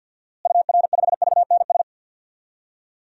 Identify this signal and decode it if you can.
Morse code, heard as a keyed tone, about 49 wpm, 700 Hz. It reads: UR5VNS
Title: DXCC-100 Call Signs - 6 Characters - Repeated 2x - 50wpm